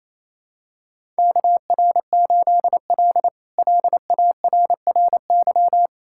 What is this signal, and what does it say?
Morse code, heard as a keyed tone, about 28 wpm, 700 Hz. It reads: KR8L LARRY
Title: K1USN SST QSO Practice - 28wpm